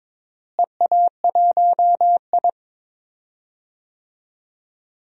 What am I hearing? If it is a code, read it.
Morse code, heard as a keyed tone, about 22 wpm, 700 Hz. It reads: EA1I
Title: DXCC-100 Call Signs - 4 Characters - Repeated 3x - 22wpm